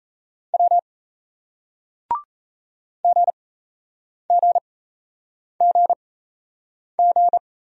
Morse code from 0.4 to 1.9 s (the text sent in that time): W